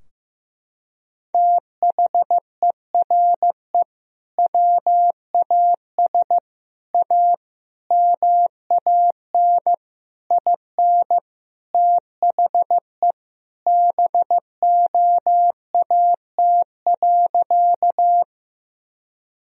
Morse code, heard as a keyed tone, about 16 wpm, 700 Hz. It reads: THERE WAS A MAN IN THE BOAT.
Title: Sentences from Top 1000 Words - Review Part 3 - 15wpm